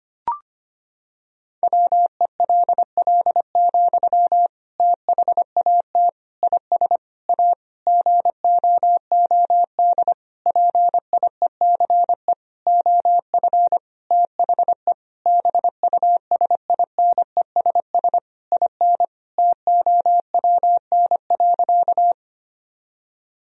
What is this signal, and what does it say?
Morse code, heard as a keyed tone, about 25 wpm, 700 Hz. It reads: WELL, THAT IS A GOOD PIECE OF THE BUSINESS IN TOWN.
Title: Sentences from Top 1000 Words - Review Part 2 - 25wpm